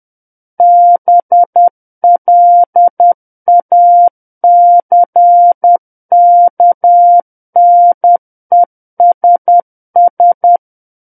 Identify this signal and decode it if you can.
Morse code, heard as a keyed tone, about 10 wpm, 700 Hz. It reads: BLACKNESS
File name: Book One / 01-Chapter 1 - 10 WPM